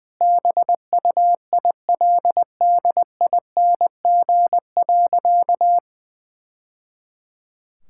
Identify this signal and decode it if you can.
Morse code, heard as a keyed tone, about 20 wpm, 700 Hz. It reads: BUILDING.
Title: Sentences from Top 1000 Words - Review Part 5 - 20wpm